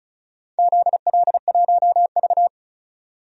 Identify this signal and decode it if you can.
Morse code, heard as a keyed tone, about 36 wpm, 700 Hz. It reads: ZL1V
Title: DXCC-100 Call Signs - 4 Characters - 35wpm